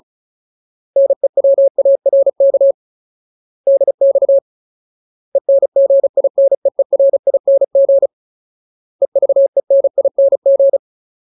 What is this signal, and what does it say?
Morse code, heard as a keyed tone, about 34 wpm, 550 Hz. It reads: NEWARK DX ENGINEERING EVENING
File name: Ultimate Word List / 35WPM Ultimate